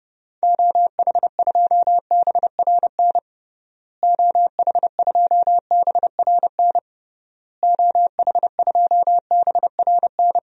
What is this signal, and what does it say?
Morse code, heard as a keyed tone, about 30 wpm, 700 Hz. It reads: OH2BRN OH2BRN OH2BRN
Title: DXCC-100 Call Signs - 6 Characters - Repeated 3x - 30wpm